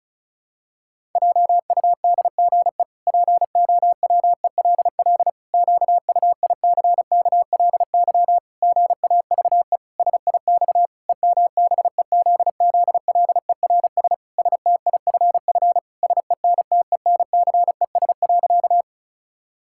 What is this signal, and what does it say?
Morse code, heard as a keyed tone, about 36 wpm, 700 Hz. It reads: JUDGE POWELL QUICKLY GAVE SIX EMBEZZLERS STIFF SENTENCES.
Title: Pangram Challenge v2 - 35wpm